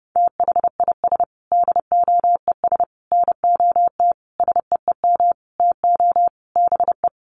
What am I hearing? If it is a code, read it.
Morse code, heard as a keyed tone, about 30 wpm, 700 Hz. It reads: THIS DOES NOT SEEM TO BE
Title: Sentences from Top 1400 Words 30wpm